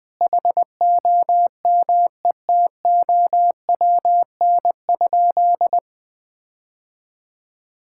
Morse code, heard as a keyed tone, about 20 wpm, 700 Hz. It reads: HOMETOWN?
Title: Sentences with 8-Word Phrases - 20wpm